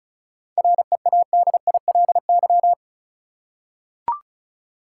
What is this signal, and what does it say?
Morse code, heard as a keyed tone, about 34 wpm, 700 Hz. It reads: READILY
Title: Seven Letter Words - 35wpm